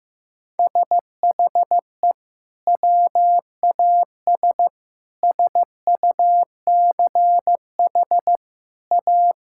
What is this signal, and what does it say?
Morse code, heard as a keyed tone, about 15 wpm, 700 Hz. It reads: SHE WAS SUCH A
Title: Sentences from Top 1600 Words 15wpm